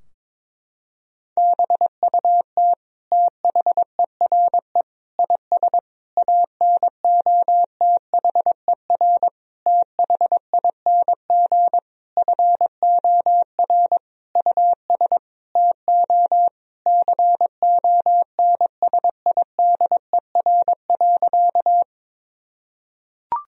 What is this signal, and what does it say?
Morse code, heard as a keyed tone, about 21 wpm, 700 Hz. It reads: BUT THERE IS ANOTHER THING FOR US TO CONSIDER.
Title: Sentences from Top 1000 Words - Review Part 4 - 22wpm